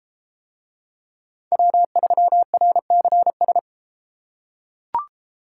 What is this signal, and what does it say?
Morse code, heard as a keyed tone, about 31 wpm, 700 Hz. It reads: W3RCS